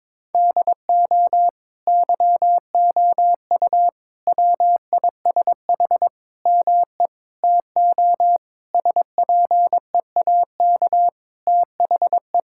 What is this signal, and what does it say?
Morse code, heard as a keyed tone, about 22 wpm, 700 Hz. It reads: DO YOU WISH ME TO SPEAK THE
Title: Sentences from Top 1600 Words 22wpm